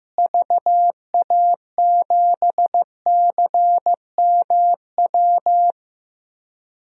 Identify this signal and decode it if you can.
Morse code, heard as a keyed tone, about 15 wpm, 700 Hz. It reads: VA7CMW
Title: DXCC-100 Call Signs - 6 Characters - Repeated 3x - 15wpm